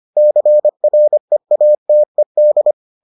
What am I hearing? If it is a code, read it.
Morse code, heard as a keyed tone, about 25 wpm, 600 Hz. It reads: CREATED